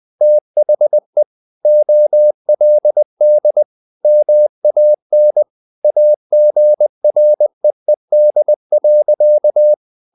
Morse code, heard as a keyed tone, about 20 wpm, 600 Hz. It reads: THE OLD MAN AGREED.
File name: the old man and the sea / the old man and the sea part 1 20wpm